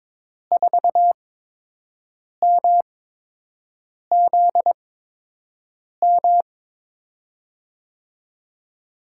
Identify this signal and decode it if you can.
Morse code, heard as a keyed tone, about 22 wpm, 700 Hz. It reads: 4 M Z M
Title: USA Call Signs - 8x Character Spacing - 22wpm